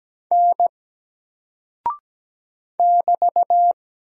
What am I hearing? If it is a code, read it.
Morse code, heard as a keyed tone, about 17 wpm, 700 Hz. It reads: N <BT>
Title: Single Character (Letter:Number:Punctuation:Prosign) 17wpm